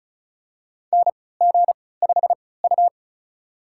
Morse code, heard as a keyed tone, about 36 wpm, spaced as for 20 wpm, 700 Hz. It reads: NG5U
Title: USA Call Signs - 4 Characters - 3x Character Spacing - 35wpm